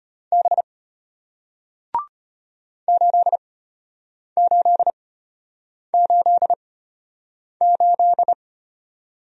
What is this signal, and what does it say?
Morse code, heard as a keyed tone, about 33 wpm, spaced as for 8 wpm, 700 Hz. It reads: B 8888